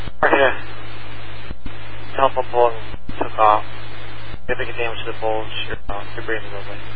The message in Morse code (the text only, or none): none